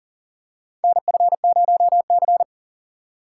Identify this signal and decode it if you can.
Morse code, heard as a keyed tone, about 40 wpm, 700 Hz. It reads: NF0C
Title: USA Call Signs - 4 Characters - 2x Character Spacing - 40wpm